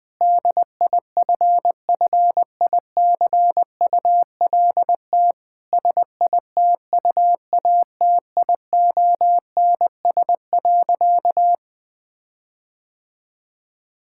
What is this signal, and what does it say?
Morse code, heard as a keyed tone, about 21 wpm, 700 Hz. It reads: DIFFICULT SITUATIONS.
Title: Difficult Sentences with Hints - 20wpm